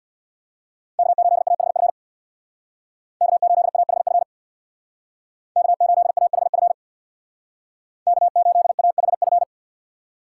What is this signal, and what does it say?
Morse code, heard as a keyed tone, about 61 wpm, 700 Hz. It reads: K8AHF K8AHF K8AHF K8AHF
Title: Speed Racing - USA Call Signs 50wpm